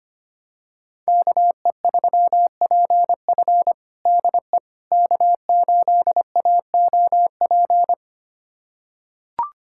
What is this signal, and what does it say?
Morse code, heard as a keyed tone, about 25 wpm, 700 Hz. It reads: KE3PF DE K8AOP
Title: USA Callsign DE Callsign - 5 Characters - 25wpm